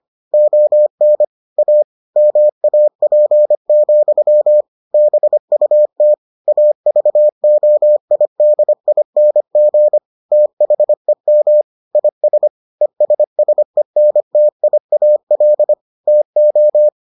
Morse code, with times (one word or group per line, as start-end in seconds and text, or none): 0.3-1.2 ON
1.6-1.8 A
2.2-4.6 MAP,
4.9-6.1 BUT
6.5-10.0 AVOIDING
10.3-11.6 THEM
11.9-12.5 IS
12.8-15.7 ESSENTIAL
16.1-16.9 TO